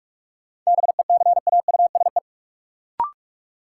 Morse code, heard as a keyed tone, about 46 wpm, 700 Hz. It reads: BECAUSE
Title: Sentences from Top 600 Words - New Word Drill - 45wpm